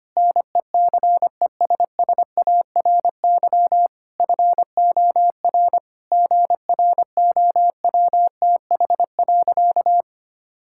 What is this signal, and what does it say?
Morse code, heard as a keyed tone, about 25 wpm, 700 Hz. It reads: NECESSARY FOR GROWTH.
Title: Sentences from Top 1000 Words - Review Part 5 - 25wpm